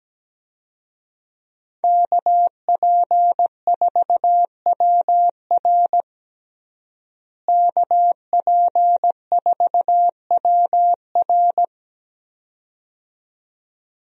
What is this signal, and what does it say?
Morse code, heard as a keyed tone, about 17 wpm, 700 Hz. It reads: KP4WR KP4WR
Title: USA Call Signs - 5 Characters - Repeated 2x - 17wpm